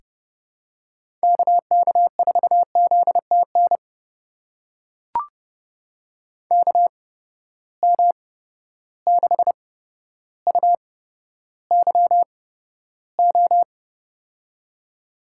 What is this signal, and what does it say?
Morse code, heard as a keyed tone, about 30 wpm, spaced as for 13 wpm, 700 Hz. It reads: KK4ZTN KM6UYO
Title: USA Call Signs - 6 Characters - 8x Character Spacing - 30wpm